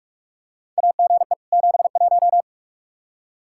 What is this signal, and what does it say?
Morse code, heard as a keyed tone, about 46 wpm, 700 Hz. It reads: AGE 71
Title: Practice QSO - Exchange 3 - More Info - 45wpm